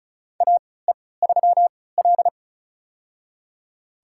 Morse code, heard as a keyed tone, about 36 wpm, spaced as for 20 wpm, 700 Hz. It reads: AE3L
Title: USA Call Signs - 4 Characters - 3x Character Spacing - 35wpm